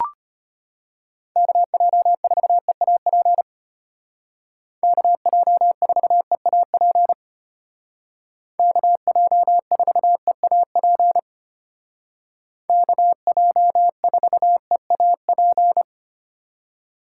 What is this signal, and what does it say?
Morse code, heard as a keyed tone, about 30 wpm, 700 Hz. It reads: KJ4EAP KJ4EAP KJ4EAP KJ4EAP
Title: Speed Racing - USA 2x3 Call Signs 25wpm